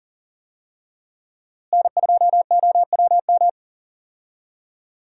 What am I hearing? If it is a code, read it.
Morse code, heard as a keyed tone, about 40 wpm, 700 Hz. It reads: N2OWM
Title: USA Call Signs - 5 Characters - Repeated 3x - 40wpm